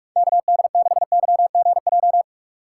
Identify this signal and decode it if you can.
Morse code, heard as a keyed tone, about 46 wpm, 700 Hz. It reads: KD6YGJ
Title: USA Call Signs - 3x Character Spacing - 45wpm